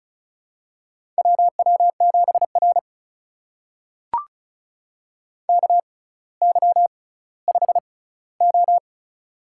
Morse code, heard as a keyed tone, about 36 wpm, spaced as for 18 wpm, 700 Hz. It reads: WW7R KY5O
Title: USA Call Signs - 4 Characters - 6x Character Spacing - 35wpm